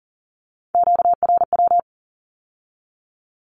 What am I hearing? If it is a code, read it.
Morse code, heard as a keyed tone, about 42 wpm, 700 Hz. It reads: QRW
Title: Q-Codes with Spoken Meaning - 40wpm